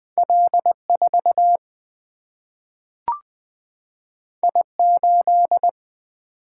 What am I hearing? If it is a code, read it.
Morse code, heard as a keyed tone, about 20 wpm, 700 Hz. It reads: L 4 I 8
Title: Single-Letter Single-Number 20wpm